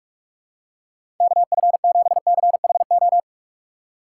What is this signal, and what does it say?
Morse code, heard as a keyed tone, about 46 wpm, 700 Hz. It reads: KF7CHO
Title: USA Call Signs - 6 Characters - 6x Character Spacing - 45wpm